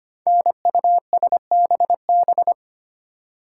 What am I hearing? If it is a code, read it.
Morse code, heard as a keyed tone, about 25 wpm, 700 Hz. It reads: NUSBB
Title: Code Groups - 5 Letters - 25wpm